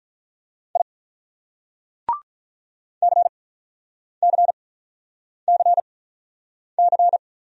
Morse code, heard as a keyed tone, about 43 wpm, spaced as for 9 wpm, 700 Hz. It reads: I CCCC
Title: Speed-Racing Single Letters 35wpm